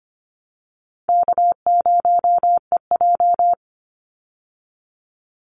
Morse code, heard as a keyed tone, about 25 wpm, 700 Hz. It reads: K0EJ
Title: Easy Call Signs Repeated 3x - 25wpm